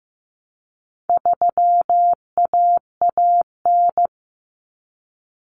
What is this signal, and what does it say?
Morse code, heard as a keyed tone, about 15 wpm, 700 Hz. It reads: 3AAN
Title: Common Cut Numbers - 4-Digit 15wpm